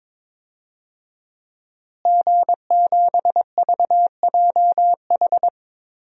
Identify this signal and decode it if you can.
Morse code, heard as a keyed tone, about 22 wpm, 700 Hz. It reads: G7VJH
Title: DXCC-100 Call Signs - 5 Characters - Repeated 3x - 22wpm